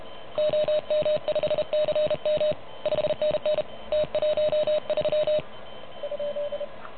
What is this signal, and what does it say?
Morse code, heard as a keyed tone, about 30 wpm, 590 Hz. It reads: OM5CM 5NN T13 ?
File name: truSDX / MWC29.8